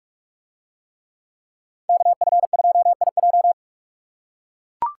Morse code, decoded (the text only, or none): KF2IJ